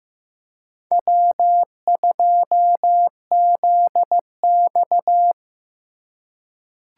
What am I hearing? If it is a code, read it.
Morse code, heard as a keyed tone, about 15 wpm, 700 Hz. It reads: W2ZX
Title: USA Call Signs - 4 Characters - 4x Character Spacing - 15wpm